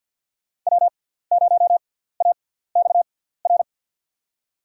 Morse code, heard as a keyed tone, about 52 wpm, 700 Hz. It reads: W 0 A X R
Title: USA Call Signs - 6x Character Spacing - 50wpm